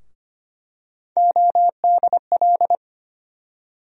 Morse code, heard as a keyed tone, about 25 wpm, 700 Hz. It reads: ODL
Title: USA Call Sign Postfixes - 3 Letters 25wpm